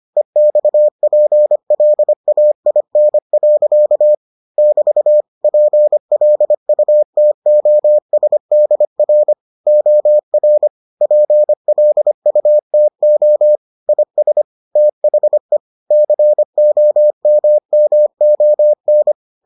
Morse code, heard as a keyed tone, about 25 wpm, 600 Hz. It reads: EXPLAIN. <BT> PLUTOSDR OR PLUTO IS THE COMMON